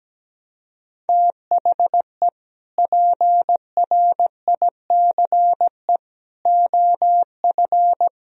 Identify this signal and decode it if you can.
Morse code, heard as a keyed tone, about 17 wpm, 700 Hz. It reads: THE PRICE OF
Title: Sentences from Top 1000 Words - Review Part 4 - 17wpm